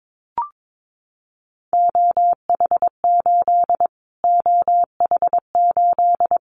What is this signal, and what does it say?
Morse code, heard as a keyed tone, about 22 wpm, 700 Hz. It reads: OH8 OH8
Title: Callsign Prefixes for Top 25 DXCC - Repeated 3x - 22wpm